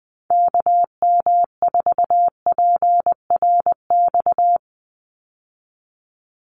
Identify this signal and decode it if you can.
Morse code, heard as a keyed tone, about 19 wpm, 700 Hz. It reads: KM4PRX
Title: USA Call Signs 20wpm